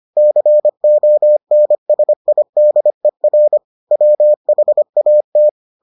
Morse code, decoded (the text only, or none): CONSIDER WHAT